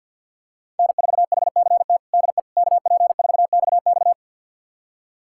Morse code, heard as a keyed tone, about 51 wpm, 700 Hz. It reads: N4HCT DE KP4XX
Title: USA Callsign DE Callsign - 5 Characters - 50wpm